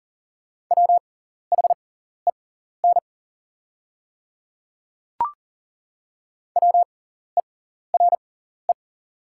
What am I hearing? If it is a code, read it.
Morse code, heard as a keyed tone, about 40 wpm, spaced as for 14 wpm, 700 Hz. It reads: WHEN WERE
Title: 6x Character Spacing - Top 100 Words - 40wpm